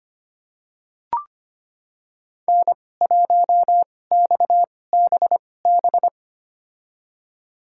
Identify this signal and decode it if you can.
Morse code, heard as a keyed tone, about 25 wpm, spaced as for 18 wpm, 700 Hz. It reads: N1XBB
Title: USA Call Signs - 5 Characters - 2x Character Spacing - 25wpm